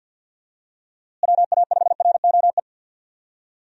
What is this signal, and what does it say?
Morse code, heard as a keyed tone, about 49 wpm, 700 Hz. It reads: WA5ROE
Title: USA Call Signs - 6 Characters - Repeated 3x - 50wpm